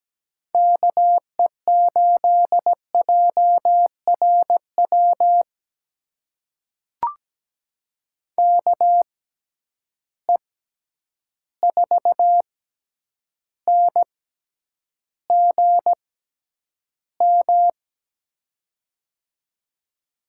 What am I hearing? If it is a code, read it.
Morse code, heard as a keyed tone, about 17 wpm, spaced as for 9 wpm, 700 Hz. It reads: KE8JRW KE4NGM
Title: USA Call Signs - 6 Characters - 6x Character Spacing - 17wpm